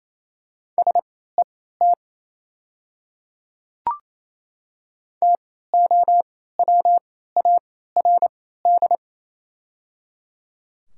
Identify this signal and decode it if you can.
Morse code, heard as a keyed tone, about 28 wpm, spaced as for 16 wpm, 700 Hz. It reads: SET TOWARD
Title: 3x Character Spacing - Top 500 Words - 28wpm